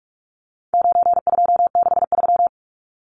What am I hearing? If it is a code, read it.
Morse code, heard as a keyed tone, about 44 wpm, 700 Hz. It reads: 9263
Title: Numbers - 4-Digit 45wpm